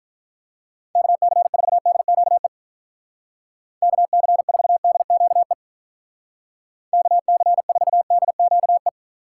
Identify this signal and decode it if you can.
Morse code, heard as a keyed tone, about 44 wpm, 700 Hz. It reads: KC4DQE KC4DQE KC4DQE